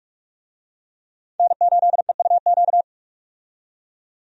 Morse code, heard as a keyed tone, about 42 wpm, 700 Hz. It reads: N8EUQ
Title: USA Call Signs - 5 Characters - 45wpm